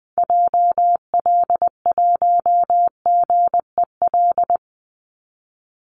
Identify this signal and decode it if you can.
Morse code, heard as a keyed tone, about 20 wpm, 700 Hz. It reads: JL1GEL
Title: Call Signs Repeated 3x - 20wpm